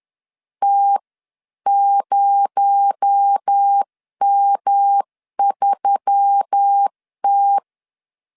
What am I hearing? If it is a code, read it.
Morse code, heard as a keyed tone, about 10 wpm, 800 Hz. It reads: T0M3T
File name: CW / DET / 1 6 1500C S C 10wpm 4252